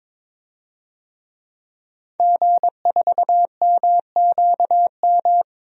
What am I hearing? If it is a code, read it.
Morse code, heard as a keyed tone, about 23 wpm, 700 Hz. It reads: G4MQM
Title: DXCC-100 Call Signs - 5 Characters - Repeated 3x - 22wpm